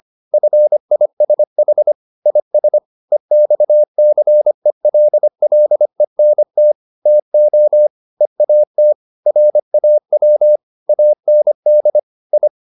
Morse code, heard as a keyed tone, about 25 wpm, 600 Hz. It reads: FISH IS EXCELLENT TO EAT RAW AND I
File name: the old man and the sea part 1 25wpm